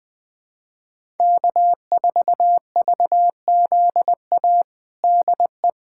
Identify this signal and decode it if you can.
Morse code, heard as a keyed tone, about 20 wpm, 700 Hz. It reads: K4VZA DE
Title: USA Callsign DE Callsign - 5 Characters - 20wpm